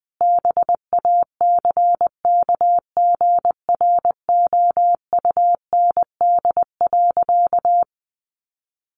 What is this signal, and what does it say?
Morse code, heard as a keyed tone, about 20 wpm, 700 Hz. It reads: BACKGROUND.
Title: Sentences from Top 1500 Words 20wpm